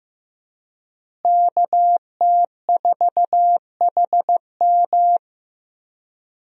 Morse code, heard as a keyed tone, about 16 wpm, 700 Hz. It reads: KT4HM
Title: USA Call Signs - 5 Characters - Repeated 2x - 15wpm